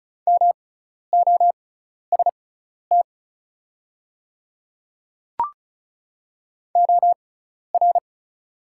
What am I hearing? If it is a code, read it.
Morse code, heard as a keyed tone, about 34 wpm, spaced as for 12 wpm, 700 Hz. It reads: MOST OR
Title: 6x Character Spacing - Top 200 Words - 35wpm